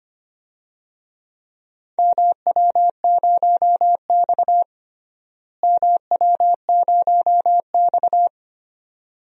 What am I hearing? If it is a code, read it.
Morse code, heard as a keyed tone, about 25 wpm, 700 Hz. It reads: MW0X MW0X
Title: DXCC-100 Call Signs - 4 Characters - Repeated 2x - 25wpm